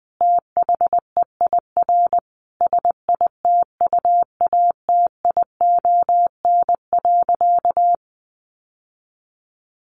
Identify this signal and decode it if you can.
Morse code, heard as a keyed tone, about 20 wpm, 700 Hz. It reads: THEIR SITUATION.
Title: Sentences from Top 1200 Words 20wpm